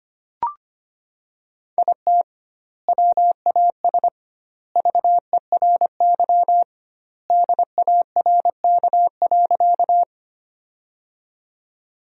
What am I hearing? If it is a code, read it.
Morse code, heard as a keyed tone, about 25 wpm, 700 Hz. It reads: IT WAS VERY DARK.
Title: 2x Word-Spacing - Sentences from Top 500 Words - 25wpm